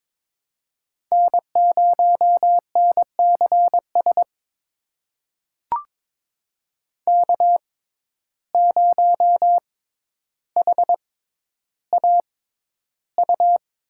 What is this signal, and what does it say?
Morse code, heard as a keyed tone, about 22 wpm, spaced as for 11 wpm, 700 Hz. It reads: N0NCS K0HAU
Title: USA Call Signs - 5 Characters - 6x Character Spacing - 22wpm